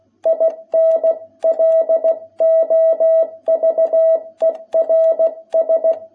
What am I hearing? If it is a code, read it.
Morse code, heard as a keyed tone, about 16 wpm, 660 Hz. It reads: IN LOVERS
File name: jh8cbh / 2021cw